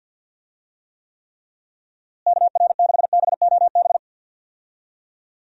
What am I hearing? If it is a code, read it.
Morse code, heard as a keyed tone, about 52 wpm, 700 Hz. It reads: KR6BOB
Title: USA Call Signs - 6 Characters - Repeated 3x - 50wpm